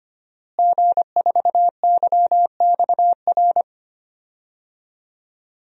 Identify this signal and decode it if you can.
Morse code, heard as a keyed tone, about 25 wpm, 700 Hz. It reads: G4YXR
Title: DXCC-100 Call Signs - 5 Characters - Repeated 2x - 25wpm